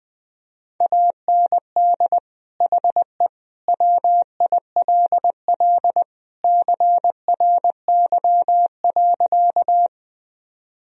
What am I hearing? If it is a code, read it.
Morse code, heard as a keyed tone, about 21 wpm, 700 Hz. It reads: AND HE WILL CRY.
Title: Sentences from Top 1000 Words - Review Part 2 - 20wpm